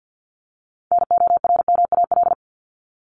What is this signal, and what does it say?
Morse code, heard as a keyed tone, about 50 wpm, 700 Hz. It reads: NORMAL
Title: LY Suffix - Top 100 Words with Base Words 50wpm